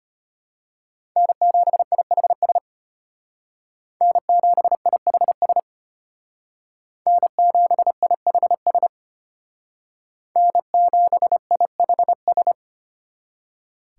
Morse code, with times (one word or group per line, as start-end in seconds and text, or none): 1.2-2.6 N7IHS
4.0-5.6 N7IHS
7.1-8.9 N7IHS
10.4-12.5 N7IHS